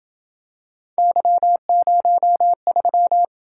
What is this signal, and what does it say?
Morse code, heard as a keyed tone, about 27 wpm, 700 Hz. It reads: Y03